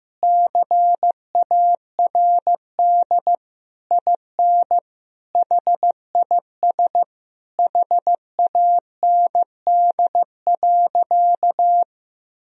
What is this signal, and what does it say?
Morse code, heard as a keyed tone, about 15 wpm, 700 Hz. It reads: CARD IN HIS HAND.
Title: Sentences from Top 1000 Words - Review Part 4 - 15wpm